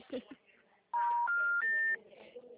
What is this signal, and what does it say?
Special information tone, with speech.
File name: out- -2039- - - .60247